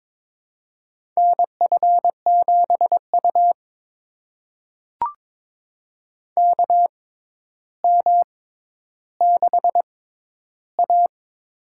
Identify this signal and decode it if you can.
Morse code, heard as a keyed tone, about 22 wpm, spaced as for 11 wpm, 700 Hz. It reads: NF7U KM6A
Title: USA Call Signs - 4 Characters - 6x Character Spacing - 22wpm